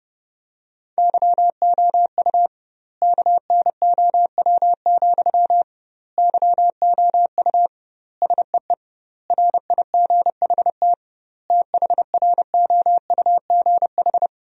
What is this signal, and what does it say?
Morse code, heard as a keyed tone, about 30 wpm, 700 Hz. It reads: YOU KNOW, YOU SEE RIGHT THROUGH
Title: 2x Word-Spacing - Sentences from Top 200 Words - 30wpm